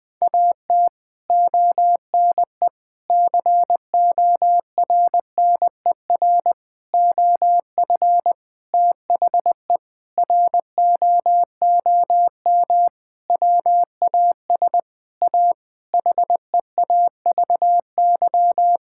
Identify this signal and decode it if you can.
Morse code, heard as a keyed tone, about 21 wpm, 700 Hz. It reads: AT ONE CORNER OF THE ROOM WAS A HEAVY
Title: Sentences from Top 1600 Words 20wpm